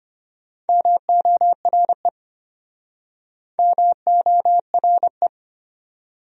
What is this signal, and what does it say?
Morse code, heard as a keyed tone, about 32 wpm, 700 Hz. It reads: MORE MORE